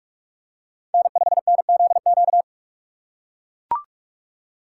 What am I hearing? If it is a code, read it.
Morse code, heard as a keyed tone, about 42 wpm, 700 Hz. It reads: N5NZQ
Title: USA Call Signs - 5 Characters - 2x Character Spacing - 45wpm